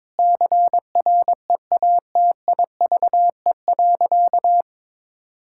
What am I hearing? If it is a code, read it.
Morse code, heard as a keyed tone, about 22 wpm, 700 Hz. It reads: CREATIVE.
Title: Sentences from Top 1000 Words - Review Part 5 - 22wpm